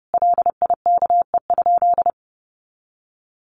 Morse code, heard as a keyed tone, about 30 wpm, 700 Hz. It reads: LIKE?
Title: Sentences from Top 1500 Words 30wpm